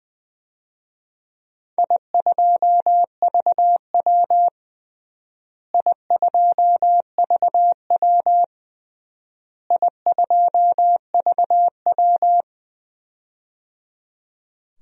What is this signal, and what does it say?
Morse code, heard as a keyed tone, about 20 wpm, 700 Hz. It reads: I2VW I2VW I2VW
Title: DXCC-100 Call Signs - 4 Characters - Repeated 3x - 20wpm